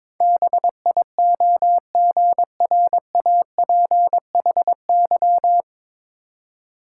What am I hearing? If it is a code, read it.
Morse code, heard as a keyed tone, about 22 wpm, 700 Hz. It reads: BIOGRAPHY
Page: Morse Code Ninja practice 22wpm